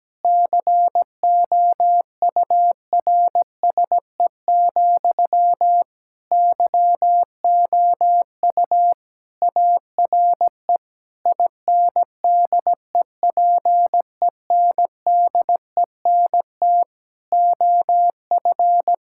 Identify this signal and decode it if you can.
Morse code, heard as a keyed tone, about 17 wpm, 700 Hz. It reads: COURSE, YOU ARE INDEPENDENT OF
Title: Sentences from Top 1000 Words - Review Part 6 - 17wpm